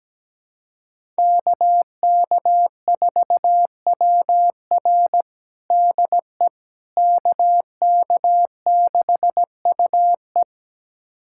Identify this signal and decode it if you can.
Morse code, heard as a keyed tone, about 17 wpm, 700 Hz. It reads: KK4WR DE KK6UE
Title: USA Callsign DE Callsign - 5 Characters - 17wpm